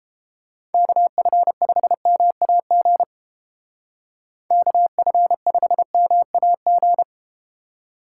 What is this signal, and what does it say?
Morse code, heard as a keyed tone, about 32 wpm, 700 Hz. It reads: KF5MAG KF5MAG